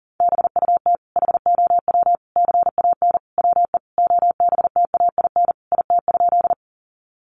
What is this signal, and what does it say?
Morse code, heard as a keyed tone, about 40 wpm, 700 Hz. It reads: BUT HOW CAN WE OBTAIN IT?
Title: Sentences from Top 1300 Words 40wpm